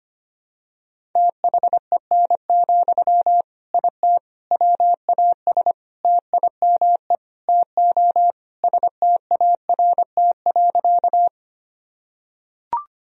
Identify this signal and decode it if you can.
Morse code, heard as a keyed tone, about 25 wpm, 700 Hz. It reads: THEN, IT WAS TIME TO START.
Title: Sentences from Top 1000 Words - Review Part 1 - 25wpm